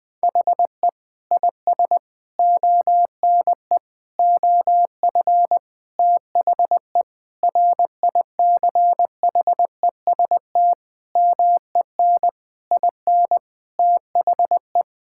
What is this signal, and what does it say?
Morse code, heard as a keyed tone, about 20 wpm, 700 Hz. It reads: HE IS ONE OF THE RICHEST MEN IN THE
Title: Sentences with 8-Word Phrases - 20wpm